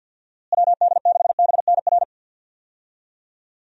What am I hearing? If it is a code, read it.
Morse code, heard as a keyed tone, about 52 wpm, 700 Hz. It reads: WD6BNR
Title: USA Call Signs - 6 Characters - Repeated 2x - 50wpm